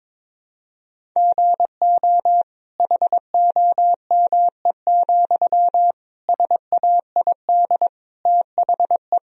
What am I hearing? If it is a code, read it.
Morse code, heard as a keyed tone, about 21 wpm, 700 Hz. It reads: GO HOME, SAID THE
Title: Sentences from Top 1000 Words - Review Part 1 - 22wpm